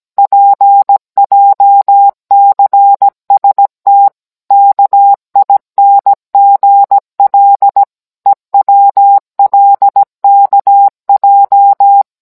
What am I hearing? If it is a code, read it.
Morse code, heard as a keyed tone, about 17 wpm, 800 Hz. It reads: PJCST KINGL EWLKJ